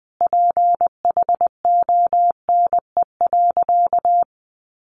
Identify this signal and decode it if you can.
Morse code, heard as a keyed tone, about 19 wpm, 700 Hz. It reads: PHONE.
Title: Sentences from Top 1100 Words 20wpm